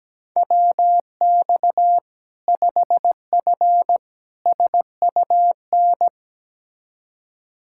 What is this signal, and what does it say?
Morse code, heard as a keyed tone, about 17 wpm, 700 Hz. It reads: WX 5F SUN
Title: Practice QSO - Exchange 2 - Info - 17wpm